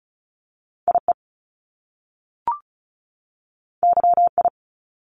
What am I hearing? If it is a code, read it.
Morse code, heard as a keyed tone, about 36 wpm, 700 Hz. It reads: IE YI
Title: Top Two Letter Combinations in Words 35wpm